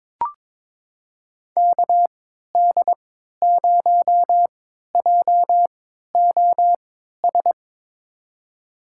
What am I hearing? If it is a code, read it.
Morse code, heard as a keyed tone, about 22 wpm, spaced as for 13 wpm, 700 Hz. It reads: KD0JOS
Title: USA Call Signs - 6 Characters - 3x Character Spacing - 22wpm